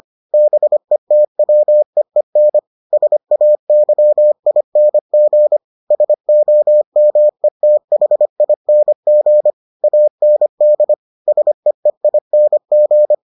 Morse code, heard as a keyed tone, about 25 wpm, 600 Hz. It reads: BETWEEN SAYING SOMETHING AND SEEING